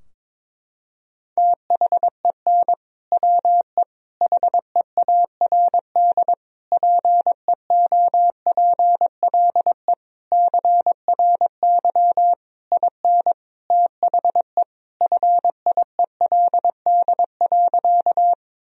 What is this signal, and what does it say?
Morse code, heard as a keyed tone, about 22 wpm, 700 Hz. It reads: THEN WE HEARD PEOPLE CRY IN THE FIELD.
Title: Sentences from Top 1000 Words - Review Part 2 - 22wpm